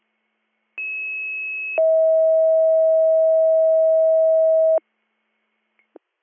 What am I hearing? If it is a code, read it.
Two-tone page, heard as a one-second tone, then a three-second tone, about 2600 then 640 Hz. It is heard from 0.8 to 4.8 s.